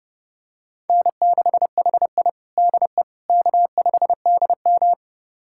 Morse code, heard as a keyed tone, about 30 wpm, 700 Hz. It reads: N6HI DE K5DM
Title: USA Callsign DE Callsign - 4 Characters - 30wpm